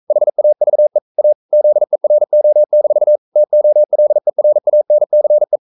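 Morse code, heard as a keyed tone, about 42 wpm, 600 Hz. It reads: HAVE A ZERO-TOLERANCE
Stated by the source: Morse code sending of an English novel at 42 wpm.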